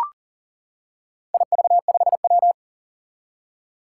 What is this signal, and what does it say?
Morse code, heard as a keyed tone, about 38 wpm, 700 Hz. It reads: IV5W
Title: Four Challenging Letter-Number Combinations 40wpm